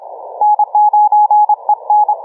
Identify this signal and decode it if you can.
Morse code, heard as a keyed tone, about 27 wpm, 840 Hz. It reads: N9EN
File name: kf6mwa / cqww cw2